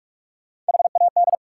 Morse code, heard as a keyed tone, about 59 wpm, 700 Hz. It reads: HAD